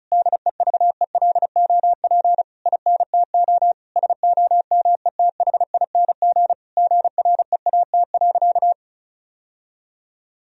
Morse code, heard as a keyed tone, about 36 wpm, 700 Hz. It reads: DEVELOP INTO SOMETHING GREAT.
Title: Sentences from Top 1000 Words - Review Part 2 - 35wpm